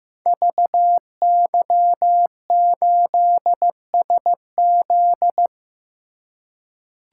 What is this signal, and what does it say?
Morse code, heard as a keyed tone, about 15 wpm, 700 Hz. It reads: VY8SZ
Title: Code Groups - 5 Letter:Numbers - 15wpm